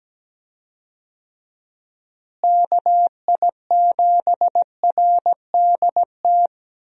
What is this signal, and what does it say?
Morse code, heard as a keyed tone, about 17 wpm, 700 Hz. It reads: KI7RDT
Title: USA Call Signs - 6 Characters - Repeated 2x - 17wpm